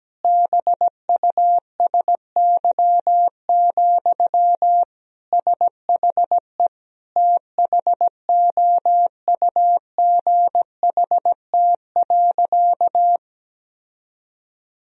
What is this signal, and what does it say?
Morse code, heard as a keyed tone, about 17 wpm, 700 Hz. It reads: BUSY, SHE THOUGHT.
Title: Sentences from Top 1000 Words - Review Part 3 - 17wpm